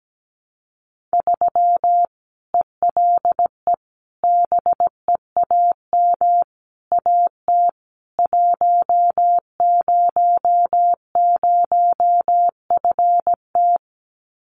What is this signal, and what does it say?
Morse code, heard as a keyed tone, about 17 wpm, 700 Hz. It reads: 3 ELE BEAM AT 100FT
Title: Common QSO Phrases 17wpm